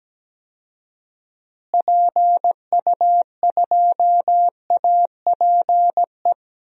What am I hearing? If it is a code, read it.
Morse code, heard as a keyed tone, about 17 wpm, 700 Hz. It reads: PU2APE
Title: DXCC-100 Call Signs - 6 Characters - Repeated 3x - 17wpm